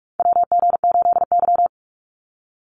Morse code, heard as a keyed tone, about 44 wpm, 690 Hz. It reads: WG8Y
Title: NA SOTA Chaser Callsigns 45wpm